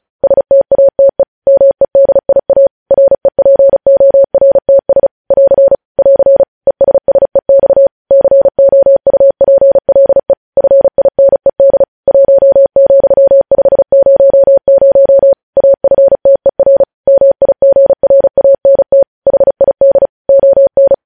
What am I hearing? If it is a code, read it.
Morse code, heard as a keyed tone, about 36 wpm, 550 Hz. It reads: STATE MEDIA REPORTS <AR> <AR> ESSEX COUPLE FINED 1,500 AFTER MIGRANT HID ON
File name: podcast / News headlines-1x-35x35